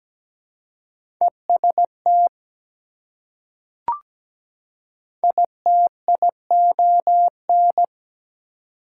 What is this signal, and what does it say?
Morse code, heard as a keyed tone, about 17 wpm, 700 Hz. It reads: EST ITION
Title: Prefixes and Suffixes 17wpm